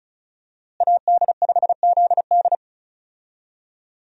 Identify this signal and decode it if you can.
Morse code, heard as a keyed tone, about 34 wpm, 700 Hz. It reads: AD5ZD
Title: USA Call Signs - 5 Characters - Repeated 3x - 35wpm